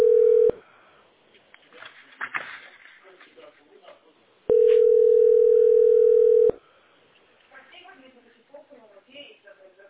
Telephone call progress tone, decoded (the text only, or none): ringback tone